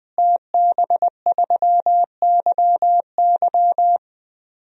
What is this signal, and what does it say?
Morse code, heard as a keyed tone, about 21 wpm, 700 Hz. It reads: TB3YY
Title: DXCC-100 Call Signs - 5 Characters - Repeated 2x - 20wpm